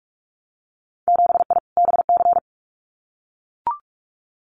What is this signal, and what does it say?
Morse code, heard as a keyed tone, about 44 wpm, 700 Hz. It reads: 7I BC
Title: Winter Field Day Exchanges Encore 45wpm